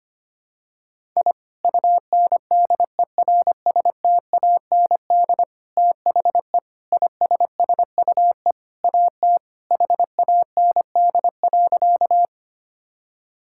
Morse code, heard as a keyed tone, about 25 wpm, 700 Hz. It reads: I UNDERSTAND THE ISSUE AT HAND.
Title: Sentences from Top 1000 Words - Review Part 6 - 25wpm